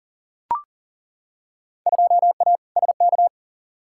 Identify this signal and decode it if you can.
Morse code, heard as a keyed tone, about 40 wpm, 700 Hz. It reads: 2A SK
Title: Field Day Exchanges Encore - 40wpm